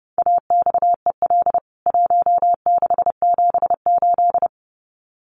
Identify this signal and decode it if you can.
Morse code, heard as a keyed tone, about 30 wpm, 700 Hz. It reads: AXEL 1678
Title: CWT Exchange 30wpm